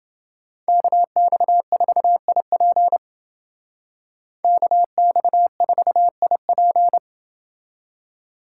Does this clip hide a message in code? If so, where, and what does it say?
Yes: Morse code from 0.7 to 7.0 s, heard as a keyed tone, about 29 wpm, 700 Hz. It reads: KX4IP KX4IP